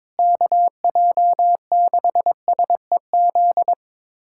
Morse code, heard as a keyed tone, about 22 wpm, 700 Hz. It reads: KJ6SEZ
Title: USA Callsign DE Callsign - 6 Characters - 22wpm